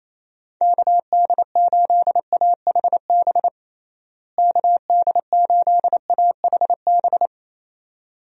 Morse code, heard as a keyed tone, about 28 wpm, 700 Hz. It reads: KD8AHB KD8AHB
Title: USA Call Signs - 6 Characters - Repeated 2x - 28wpm